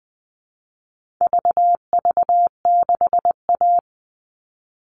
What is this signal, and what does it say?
Morse code, heard as a keyed tone, about 20 wpm, 700 Hz. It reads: VV6A
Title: All Cut Numbers - 4-Digit 20wpm